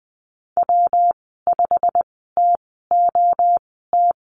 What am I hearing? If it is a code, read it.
Morse code, heard as a keyed tone, about 20 wpm, spaced as for 15 wpm, 700 Hz. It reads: W5TOT
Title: 122 Lesson 20g - Number 3 - Call Signs 20wpm